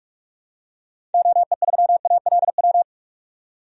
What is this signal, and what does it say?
Morse code, heard as a keyed tone, about 43 wpm, 700 Hz. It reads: OE3ALW
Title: DXCC-100 Call Signs - 6 Characters - 45wpm